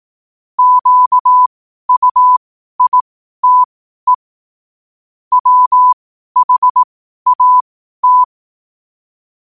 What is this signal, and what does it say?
Morse code, heard as a keyed tone, about 18 wpm, spaced as for 12 wpm, 1000 Hz. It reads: QUITE WHAT
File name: qotdfiles / 12wpm